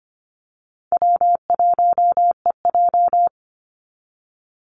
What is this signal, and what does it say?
Morse code, heard as a keyed tone, about 24 wpm, 700 Hz. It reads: W1EJ
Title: NA SOTA Activator Callsigns 25wpm